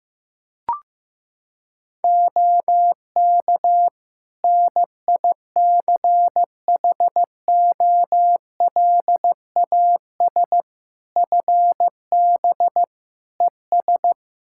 Morse code, heard as a keyed tone, about 15 wpm, 700 Hz. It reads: OK NICHOLAS FB ES
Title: Practice QSO - Exchange 2 - Info - 15wpm